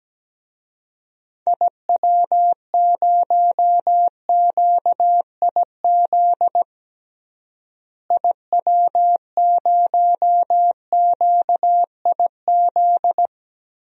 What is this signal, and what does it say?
Morse code, heard as a keyed tone, about 17 wpm, 700 Hz. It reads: IW0QIZ IW0QIZ
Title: DXCC-100 Call Signs - 6 Characters - Repeated 3x - 17wpm